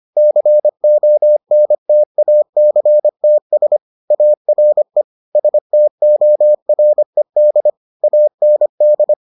Morse code, heard as a keyed tone, about 25 wpm, 600 Hz. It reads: CONTACTS ARE STORED AND